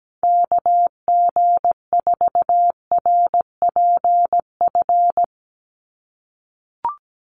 Morse code, heard as a keyed tone, about 17 wpm, 700 Hz. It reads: KG4RPF
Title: USA Call Signs 17wpm